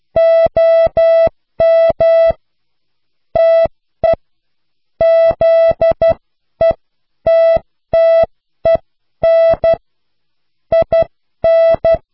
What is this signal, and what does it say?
Morse code, heard as a keyed tone, about 12 wpm, spaced as for 10 wpm, 660 Hz. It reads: OM TE ZETTEN IN